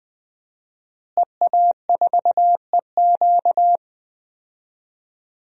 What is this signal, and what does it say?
Morse code, heard as a keyed tone, about 20 wpm, 700 Hz. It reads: EA4EQ
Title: DXCC-100 Call Signs - 5 Characters - Repeated 3x - 20wpm